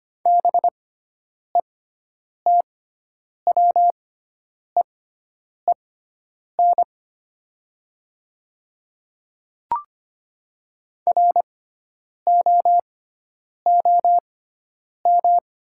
Morse code, heard as a keyed tone, about 25 wpm, spaced as for 9 wpm, 700 Hz. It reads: BETWEEN ROOM
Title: 6x Character Spacing - Top 300 Words - 25wpm